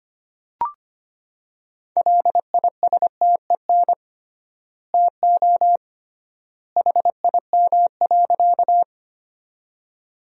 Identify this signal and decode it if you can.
Morse code, heard as a keyed tone, about 25 wpm, 700 Hz. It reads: LISTEN TO HIM.
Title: 3x Word-Spacing - Sentences from Top 400 Words - 25wpm